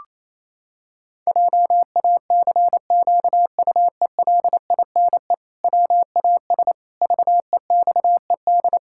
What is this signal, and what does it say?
Morse code, heard as a keyed tone, about 28 wpm, 700 Hz. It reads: JACQUELINE WAS VEXED
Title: Pangram Challenge v2 - 28wpm